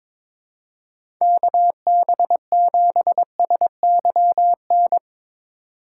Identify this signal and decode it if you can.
Morse code, heard as a keyed tone, about 22 wpm, 700 Hz. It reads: KB7SYN
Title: USA Call Signs - 4x Character Spacing - 22wpm